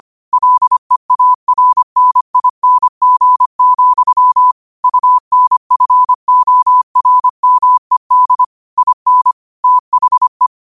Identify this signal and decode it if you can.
Morse code, heard as a keyed tone, about 25 wpm, 1000 Hz. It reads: LEARNING, UNFORMED IN THE